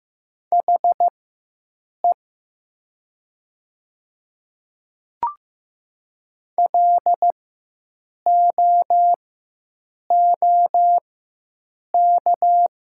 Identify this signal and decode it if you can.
Morse code, heard as a keyed tone, about 15 wpm, spaced as for 7 wpm, 700 Hz. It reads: HE LOOK
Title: 4x Character Spacing - Top 300 Words - 15wpm